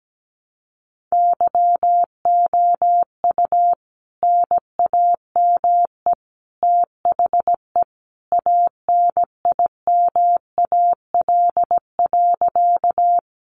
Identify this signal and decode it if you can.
Morse code, heard as a keyed tone, about 17 wpm, 700 Hz. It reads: YOU NAME THE ANIMAL.
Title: A Sound Of Thunder 17wpm